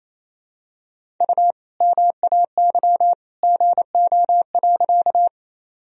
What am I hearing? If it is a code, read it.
Morse code, heard as a keyed tone, about 28 wpm, 700 Hz. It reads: U MAY GO.
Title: Sentences from Top 100 Words and Abbreviations 28wpm